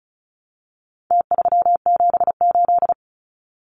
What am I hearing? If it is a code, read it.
Morse code, heard as a keyed tone, about 34 wpm, 700 Hz. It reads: T378
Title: Numbers - Contest Style - 4-Digit 35wpm